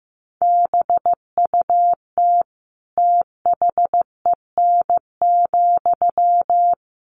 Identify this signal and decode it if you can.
Morse code, heard as a keyed tone, about 15 wpm, 700 Hz. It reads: BUT THEN,
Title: Sentences from Top 1400 Words 15wpm